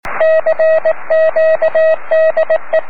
Morse code, heard as a keyed tone, about 18 wpm, 640 Hz, with radio noise. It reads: CQ DE
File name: om3pv 70mhz / zaznam / sy9vhf